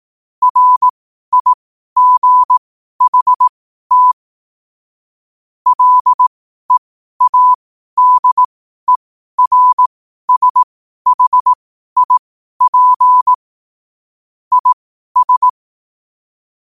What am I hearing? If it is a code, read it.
Morse code, heard as a keyed tone, about 18 wpm, spaced as for 12 wpm, 1000 Hz. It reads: RIGHT LEADERSHIP IS
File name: 2024-09-14-12wpm